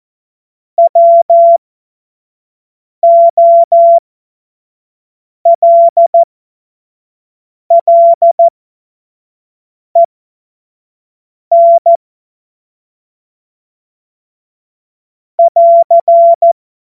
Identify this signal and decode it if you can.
Morse code, heard as a keyed tone, about 14 wpm, spaced as for 5 wpm, 680 Hz. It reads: WOLLEN <AR>
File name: Witz-15-Juli-2025-25-F70